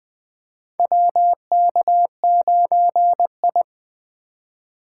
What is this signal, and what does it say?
Morse code, heard as a keyed tone, about 20 wpm, 700 Hz. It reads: WK9I
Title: USA Call Signs - 4 Characters - 4x Character Spacing - 20wpm